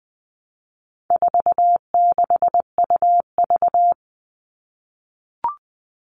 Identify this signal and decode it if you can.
Morse code, heard as a keyed tone, about 20 wpm, 700 Hz. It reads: 46UV
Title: All Cut Numbers - 4-Digit 20wpm